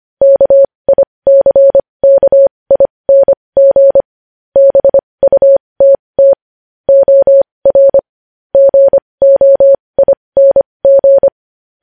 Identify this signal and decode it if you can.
Morse code, heard as a keyed tone, about 25 wpm, spaced as for 20 wpm, 550 Hz. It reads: KICKING BUTT OR GOING
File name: podcast / News headlines-1x-25x20